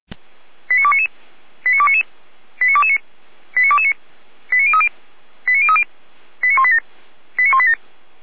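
Five-tone selective call (ZVEI), heard as quick runs of five tones, tones of about 70 ms. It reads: 89299 89299 89229 89229 89030 89030 89187 89187
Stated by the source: ZVEI five-tone call received by SDR.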